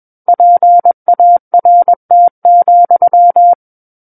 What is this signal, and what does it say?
Morse code, heard as a keyed tone, about 21 wpm, 700 Hz. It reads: PART,
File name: Book One / 12-Chapter 12 - 21 WPM